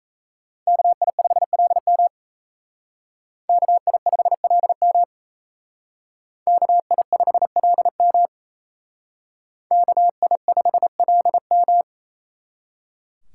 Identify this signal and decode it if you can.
Morse code, heard as a keyed tone, about 36 wpm, 700 Hz. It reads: KI5LM KI5LM KI5LM KI5LM